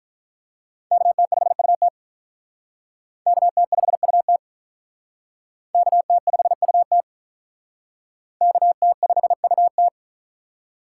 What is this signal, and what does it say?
Morse code, heard as a keyed tone, about 43 wpm, 700 Hz. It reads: KT5UT KT5UT KT5UT KT5UT
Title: Speed Racing - USA 2x2 Call Signs 35wpm